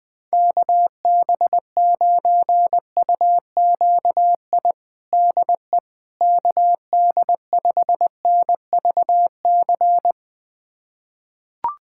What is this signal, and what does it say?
Morse code, heard as a keyed tone, about 20 wpm, 700 Hz. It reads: KB9UQI DE KD5NVC
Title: USA Callsign DE Callsign - 6 Characters - 20wpm